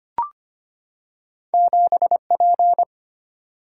Morse code, heard as a keyed tone, about 25 wpm, 700 Hz. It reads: 7P
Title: Single-Letter Single-Number 25wpm